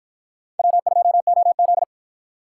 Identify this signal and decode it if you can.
Morse code, heard as a keyed tone, about 53 wpm, 700 Hz. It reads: W2OZ